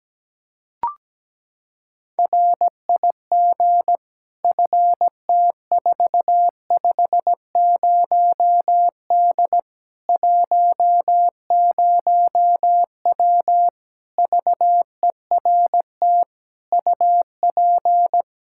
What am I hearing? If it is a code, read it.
Morse code, heard as a keyed tone, about 17 wpm, 700 Hz. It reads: RIG FT450D 10W VERT UP
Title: Practice QSO - Exchange 2 - Info - 17wpm